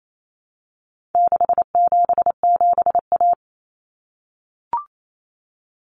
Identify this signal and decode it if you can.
Morse code, heard as a keyed tone, about 28 wpm, 700 Hz. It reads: 677A
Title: Common Cut Numbers - 4-Digit 28wpm